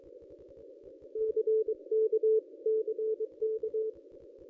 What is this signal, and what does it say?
Morse code, heard as a keyed tone, about 26 wpm, 440 Hz. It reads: CK CK
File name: contests / ss-cw-k5tr-op-n5rz -1100